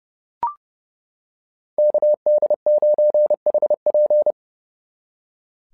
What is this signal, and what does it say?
Morse code, heard as a keyed tone, about 30 wpm, 600 Hz. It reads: KD9HP
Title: SOTA QSO Practice 30wpm